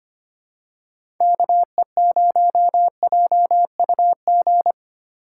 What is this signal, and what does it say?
Morse code, heard as a keyed tone, about 25 wpm, 700 Hz. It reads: KE0JUG
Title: USA Call Signs - 6 Characters - 4x Character Spacing - 25wpm